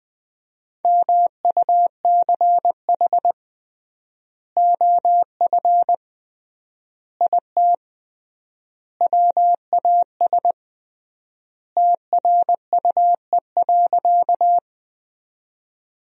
Morse code, heard as a keyed tone, about 20 wpm, 700 Hz. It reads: MUCH OF IT WAS TRUE.
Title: 3x Word-Spacing - Sentences from Top 400 Words - 20wpm